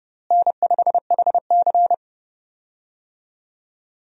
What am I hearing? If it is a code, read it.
Morse code, heard as a keyed tone, about 30 wpm, 700 Hz. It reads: N5HC
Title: USA Call Signs - 4 Characters - Repeated 3x - 30wpm